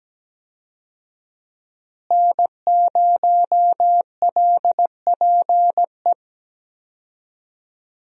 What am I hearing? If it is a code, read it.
Morse code, heard as a keyed tone, about 17 wpm, 700 Hz. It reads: N0LPE
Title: USA Call Signs - 5 Characters - Repeated 2x - 17wpm